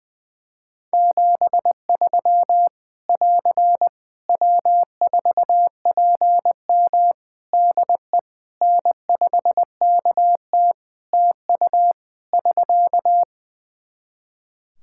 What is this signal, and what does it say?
Morse code, heard as a keyed tone, about 20 wpm, 700 Hz. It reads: 73 <AR> W4PM DE N5KT TU <SK>
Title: K1USN SST QSO Practice 20wpm